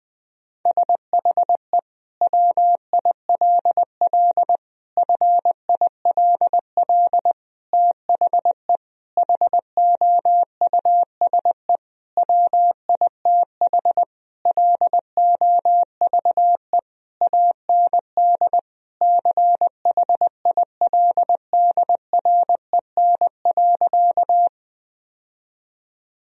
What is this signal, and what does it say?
Morse code, heard as a keyed tone, about 19 wpm, 700 Hz. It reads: SHE WILL FILL THE HOUSE WITH LOVE AND CHILDREN.
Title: Sentences from Top 1000 Words - Review Part 3 - 20wpm